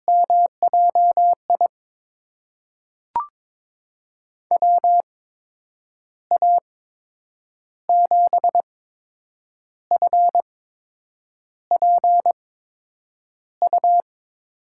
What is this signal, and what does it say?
Morse code, heard as a keyed tone, about 22 wpm, spaced as for 8 wpm, 700 Hz. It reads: MJI WA7FPU
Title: USA Call Signs - 6 Characters - 8x Character Spacing - 22wpm